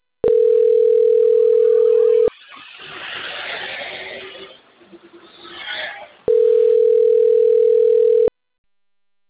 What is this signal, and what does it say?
Ringback tone (North America), 440 + 480 Hz.
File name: rg-1600-unknown